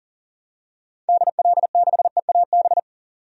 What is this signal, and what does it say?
Morse code, heard as a keyed tone, about 40 wpm, 700 Hz. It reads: DL6EAB
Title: DXCC-100 Call Signs - 6 Characters - 40wpm